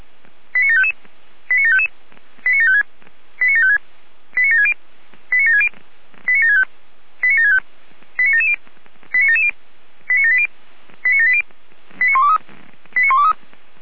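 Five-tone selective call (ZVEI), heard as quick runs of five tones, tones of about 70 ms. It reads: 89755 89755 89756 89756 89760 89760 89765 89765 89770 89770 89780 89780 89123 89123
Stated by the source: ZVEI five-tone selective call 89750 received by SDR.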